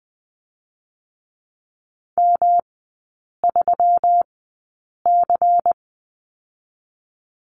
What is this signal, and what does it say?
Morse code, heard as a keyed tone, about 20 wpm, 700 Hz. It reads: M 3 C
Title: 121 Lesson 20e - Number 3 - Three Characters 20wpm